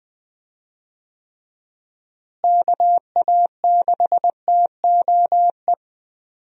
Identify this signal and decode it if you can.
Morse code, heard as a keyed tone, about 20 wpm, 700 Hz. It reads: KA6TOE
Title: USA Call Signs - 6 Characters - Repeated 2x - 20wpm